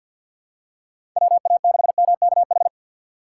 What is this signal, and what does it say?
Morse code, heard as a keyed tone, about 52 wpm, 700 Hz. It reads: WA6MKH
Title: USA Call Signs - 6 Characters - 8x Character Spacing - 50wpm